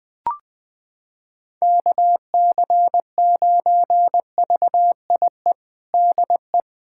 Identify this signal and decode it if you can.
Morse code, heard as a keyed tone, about 20 wpm, 700 Hz. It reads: KC9VIE DE
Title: USA Callsign DE Callsign - 6 Characters - 20wpm